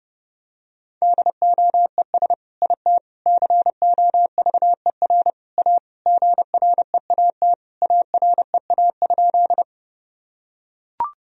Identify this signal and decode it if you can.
Morse code, heard as a keyed tone, about 30 wpm, 700 Hz. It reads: DOES IT COVER A GREAT AREA?
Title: Sentences from Top 1000 Words - Review Part 2 - 30wpm